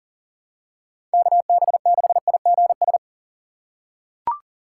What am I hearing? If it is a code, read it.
Morse code, heard as a keyed tone, about 40 wpm, 700 Hz. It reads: KB6IGS
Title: USA Call Signs - 2x Character Spacing - 40wpm